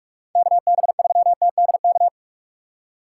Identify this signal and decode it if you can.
Morse code, heard as a keyed tone, about 44 wpm, 700 Hz. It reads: KB3TDK